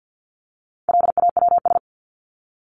Morse code, heard as a keyed tone, about 50 wpm, 700 Hz. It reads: LAWS
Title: ES:S Suffix - Top 100 Words with Base Words 50wpm